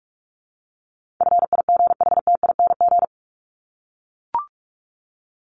Morse code, heard as a keyed tone, about 44 wpm, 700 Hz. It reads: FIGHTING
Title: ING Suffix - Top 100 Words with Base Words 45wpm